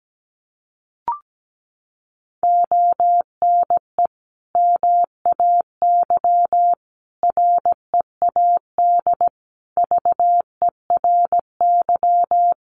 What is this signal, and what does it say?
Morse code, heard as a keyed tone, about 17 wpm, 700 Hz. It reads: ONE MAY READ VERY
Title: Sentences from Top 1100 Words 17wpm